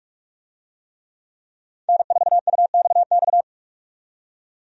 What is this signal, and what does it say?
Morse code, heard as a keyed tone, about 45 wpm, 700 Hz. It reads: N4UXX
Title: USA Call Signs - 5 Characters - 45wpm